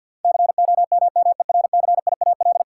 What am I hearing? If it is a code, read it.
Morse code, heard as a keyed tone, about 51 wpm, 700 Hz. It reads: COMMERCIAL
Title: Sentences from Top 1000 Words - Review Part 6 - 50wpm